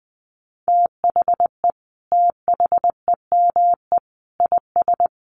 Morse code, heard as a keyed tone, about 20 wpm, 700 Hz. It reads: THE THEME IS
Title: Sentences from Top 1300 Words 20wpm